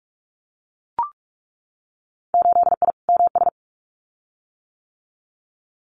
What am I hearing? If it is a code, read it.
Morse code, heard as a keyed tone, about 44 wpm, 700 Hz. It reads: 8I MS
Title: Winter Field Day Exchanges Encore 45wpm